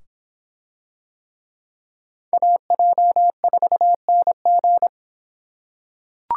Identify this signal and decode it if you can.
Morse code, heard as a keyed tone, about 24 wpm, 700 Hz. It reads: AJ4NG